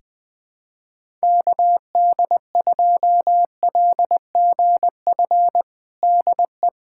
Morse code, heard as a keyed tone, about 20 wpm, 700 Hz. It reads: KD2LGF DE
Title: USA Callsign DE Callsign - 6 Characters - 20wpm